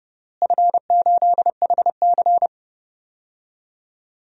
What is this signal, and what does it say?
Morse code, heard as a keyed tone, about 32 wpm, 700 Hz. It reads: F8HC
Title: DXCC-100 Call Signs - 4 Characters - Repeated 3x - 30wpm